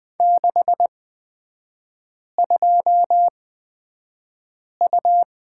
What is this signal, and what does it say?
Morse code, heard as a keyed tone, about 20 wpm, spaced as for 5 wpm, 700 Hz. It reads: 62U
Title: 255 Lesson 41e - Period - Three Characters 20wpm